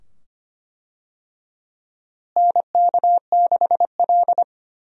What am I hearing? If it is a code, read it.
Morse code, heard as a keyed tone, about 25 wpm, 700 Hz. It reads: NK6L
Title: USA Call Signs - 4 Characters - Repeated 2x - 25wpm